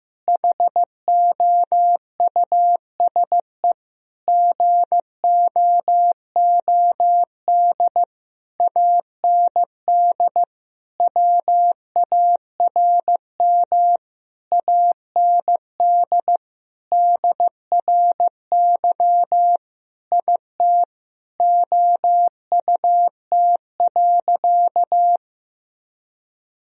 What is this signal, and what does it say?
Morse code, heard as a keyed tone, about 15 wpm, 700 Hz. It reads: HOUSE GOOD AND WARM AND DRY IT OUT.
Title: Sentences from Top 1000 Words - Review Part 3 - 15wpm